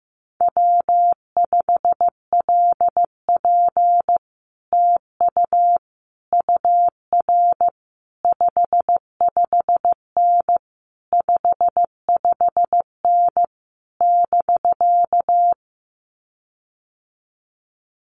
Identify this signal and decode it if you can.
Morse code, heard as a keyed tone, about 15 wpm, 700 Hz. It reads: W5LP TU UR 55N 55N <BK>
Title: Common QSO Phrases Encore 15wpm